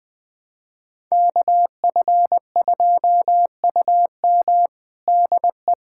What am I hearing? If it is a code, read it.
Morse code, heard as a keyed tone, about 20 wpm, 700 Hz. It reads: KF2UM DE
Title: USA Callsign DE Callsign 20wpm